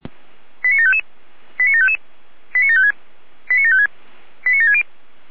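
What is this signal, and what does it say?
Five-tone selective call (ZVEI), heard as quick runs of five tones, tones of about 70 ms. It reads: 89755 89755 89756 89756 89760
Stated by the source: ZVEI five-tone selective call 89725 received by SDR.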